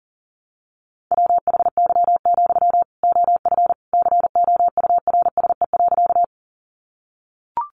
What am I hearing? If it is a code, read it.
Morse code, heard as a keyed tone, about 40 wpm, 700 Hz. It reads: WHY, OF COURSE.
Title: Sentences from Top 500 Words 40wpm